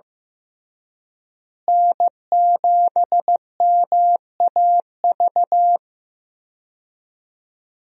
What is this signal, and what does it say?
Morse code, heard as a keyed tone, about 15 wpm, 700 Hz. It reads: N7MAV
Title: USA Call Signs - 5 Characters - Repeated 3x - 15wpm